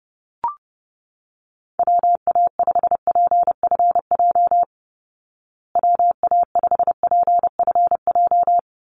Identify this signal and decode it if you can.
Morse code, heard as a keyed tone, about 30 wpm, 700 Hz. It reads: WA5PFJ WA5PFJ
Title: Call Signs Repeated 2x - 30wpm